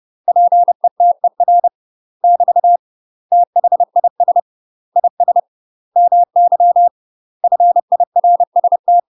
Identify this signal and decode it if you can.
Morse code, heard as a keyed tone, about 30 wpm, 700 Hz. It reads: PETER <BT> THIS IS MY FIRST
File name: UK MASTER QSOs TRAINING / 30WPM / UK MASTER QSOs 0004